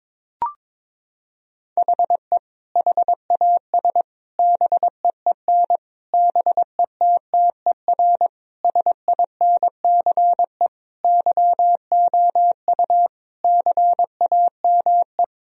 Morse code, heard as a keyed tone, about 22 wpm, 700 Hz. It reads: HE HAS BEEN BETTER SINCE YOU CAME
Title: Sentences from Top 1000 Words - Review Part 1 - 22wpm